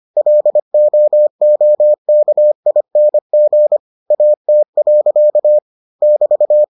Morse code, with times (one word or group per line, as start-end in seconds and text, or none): 0.2-3.8 LOOKING
4.1-5.6 AT.
6.0-6.6 <BT>